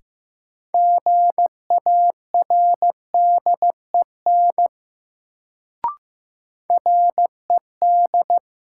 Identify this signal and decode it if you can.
Morse code, heard as a keyed tone, about 15 wpm, 700 Hz. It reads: GARDEN RED
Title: Sentences from Top 700 Words - New Word Drill - 15wpm